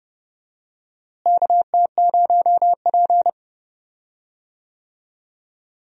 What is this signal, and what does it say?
Morse code, heard as a keyed tone, about 30 wpm, 700 Hz. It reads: KT0P
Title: USA Call Signs - 4 Characters - Repeated 3x - 30wpm